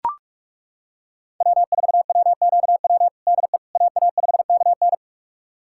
Morse code, heard as a keyed tone, about 42 wpm, 700 Hz. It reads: W4WQW DE AA5KN
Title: USA Callsign DE Callsign - 5 Characters - 45wpm